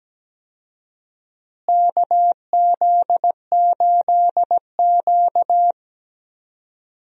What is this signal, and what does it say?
Morse code, heard as a keyed tone, about 17 wpm, 700 Hz. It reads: KZ8Q
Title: USA Call Signs - 4 Characters - Repeated 2x - 17wpm